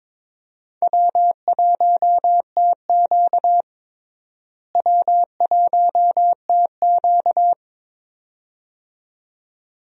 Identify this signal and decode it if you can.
Morse code, heard as a keyed tone, about 22 wpm, 700 Hz. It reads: W1TQ W1TQ
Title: USA Call Signs - 4 Characters - Repeated 2x - 22wpm